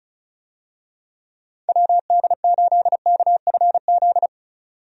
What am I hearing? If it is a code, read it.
Morse code, heard as a keyed tone, about 36 wpm, 700 Hz. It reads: WD8KFZ
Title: USA Call Signs - 6 Characters - 35wpm